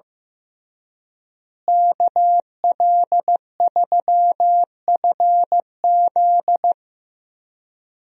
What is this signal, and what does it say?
Morse code, heard as a keyed tone, about 15 wpm, 700 Hz. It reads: KL3FZ
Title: USA Call Signs - 5 Characters - Repeated 3x - 15wpm